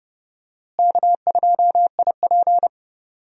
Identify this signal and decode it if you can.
Morse code, heard as a keyed tone, about 30 wpm, 700 Hz. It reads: K2IP
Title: USA Call Signs - 4 Characters - Repeated 3x - 30wpm